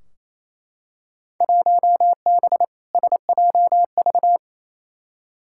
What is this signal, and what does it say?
Morse code, heard as a keyed tone, about 28 wpm, 700 Hz. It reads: 1B SJV
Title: Field Day Exchanges Encore - 28wpm